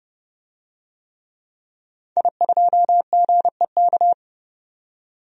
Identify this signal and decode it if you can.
Morse code, heard as a keyed tone, about 30 wpm, 690 Hz. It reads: I2GEK
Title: DXCC-100 Call Signs - 5 Characters - 30wpm